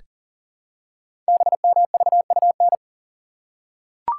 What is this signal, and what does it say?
Morse code, heard as a keyed tone, about 40 wpm, 700 Hz. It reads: BMVUN
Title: Code Groups - 5 Letters - 40wpm